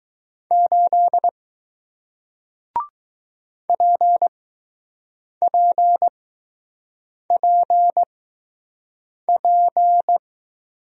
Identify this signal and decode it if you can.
Morse code, heard as a keyed tone, about 20 wpm, spaced as for 6 wpm, 700 Hz. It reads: 8 PPPP